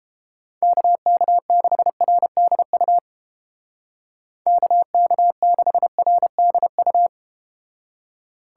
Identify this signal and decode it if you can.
Morse code, heard as a keyed tone, about 33 wpm, 700 Hz. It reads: KK6RDU KK6RDU